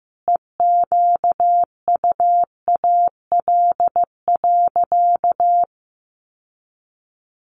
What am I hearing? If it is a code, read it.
Morse code, heard as a keyed tone, about 15 wpm, 700 Hz. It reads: EQUAL.
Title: Sentences from Top 1400 Words 15wpm